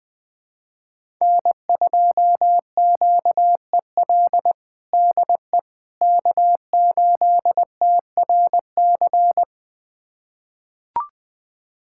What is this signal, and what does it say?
Morse code, heard as a keyed tone, about 20 wpm, 700 Hz. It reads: N2QEL DE K8TRC
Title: USA Callsign DE Callsign - 5 Characters - 20wpm